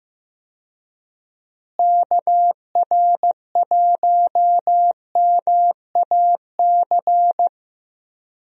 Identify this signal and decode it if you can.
Morse code, heard as a keyed tone, about 15 wpm, 700 Hz. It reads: KR1MAC
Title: USA Call Signs - 6 Characters - Repeated 3x - 15wpm